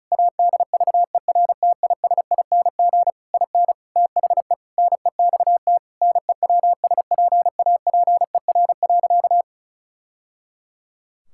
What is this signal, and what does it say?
Morse code, heard as a keyed tone, about 36 wpm, 700 Hz. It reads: ADVERTISING IN THE NEXT NEWSPAPER.
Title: Sentences from Top 1000 Words - Review Part 6 - 35wpm